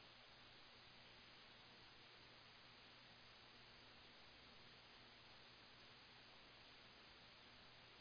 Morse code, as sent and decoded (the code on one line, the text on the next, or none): none